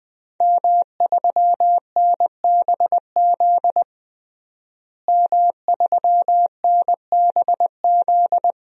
Morse code, heard as a keyed tone, about 20 wpm, 700 Hz. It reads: M3NBZ M3NBZ
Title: DXCC-100 Call Signs - 5 Characters - Repeated 2x - 20wpm